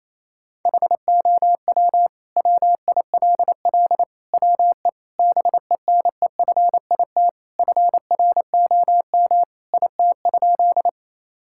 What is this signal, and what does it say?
Morse code, heard as a keyed tone, about 28 wpm, 690 Hz. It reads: HOW WILL WE BENEFIT FROM IT?
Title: Sentences from Top 1000 Words - Review Part 6 - 28wpm